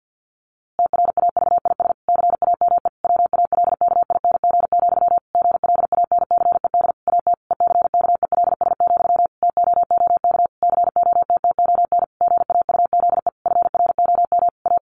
Morse code, heard as a keyed tone, about 48 wpm, 700 Hz. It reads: TRAVIS CAME WALKING, GLANCED AT ECKELS, TOOK COTTON GAUZE FROM A
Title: A Sound Of Thunder 50wpm